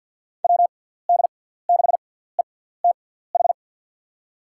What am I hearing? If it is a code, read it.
Morse code, heard as a keyed tone, about 47 wpm, spaced as for 17 wpm, 700 Hz. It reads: WD6ETH
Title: USA Call Signs - 6 Characters - 6x Character Spacing - 50wpm